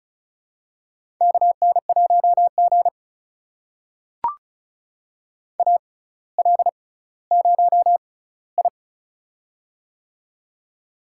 Morse code, heard as a keyed tone, about 36 wpm, spaced as for 18 wpm, 700 Hz. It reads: KN1G AL0I
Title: USA Call Signs - 4 Characters - 6x Character Spacing - 35wpm